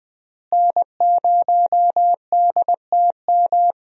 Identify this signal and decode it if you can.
Morse code, heard as a keyed tone, about 20 wpm, 700 Hz. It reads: N0DTM
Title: USA Call Signs - 5 Characters - Repeated 3x - 20wpm